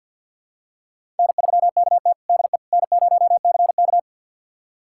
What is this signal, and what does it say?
Morse code, heard as a keyed tone, about 52 wpm, 700 Hz. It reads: N3KT DE N0CK
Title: USA Callsign DE Callsign - 4 Characters - 50wpm